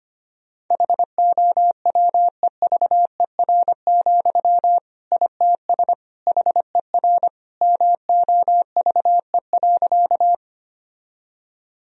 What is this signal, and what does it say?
Morse code, heard as a keyed tone, about 25 wpm, 700 Hz. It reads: HOWEVER, ITS HER MOVE.
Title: Sentences from Top 1000 Words - Review Part 1 - 25wpm